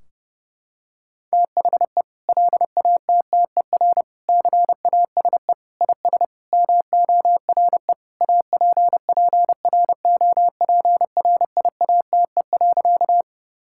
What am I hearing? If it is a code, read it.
Morse code, heard as a keyed tone, about 30 wpm, 700 Hz. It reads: THE LATTER CASE IS MORE APPROPRIATE.
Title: Sentences from Top 1600 Words 30wpm